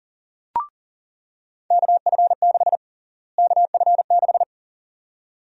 Morse code, heard as a keyed tone, about 40 wpm, 700 Hz. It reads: KF6 KF6
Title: USA Call Sign Prefixes - Repeated 2x - 40wpm